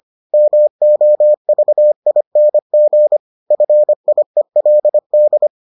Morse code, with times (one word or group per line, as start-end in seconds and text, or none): 0.3-3.2 MOVING
3.5-5.5 FIELD